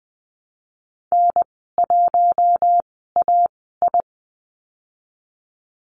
Morse code, heard as a keyed tone, about 20 wpm, spaced as for 15 wpm, 700 Hz. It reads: N1AI
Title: 048 Lesson 9g - Number 4 - Call Signs 20wpm